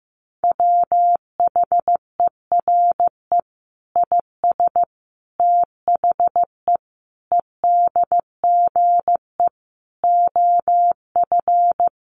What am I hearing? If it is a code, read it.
Morse code, heard as a keyed tone, about 15 wpm, 700 Hz. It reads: WHERE IS THE EDGE OF
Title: Sentences from Top 1300 Words 15wpm